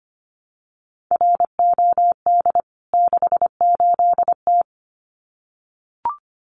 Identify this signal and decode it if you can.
Morse code, heard as a keyed tone, about 25 wpm, 700 Hz. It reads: ROD 68T
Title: CWT Exchange 25wpm